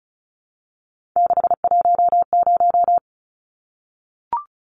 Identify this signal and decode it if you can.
Morse code, heard as a keyed tone, about 34 wpm, 700 Hz. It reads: 610
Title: Numbers - 3-Digit 35wpm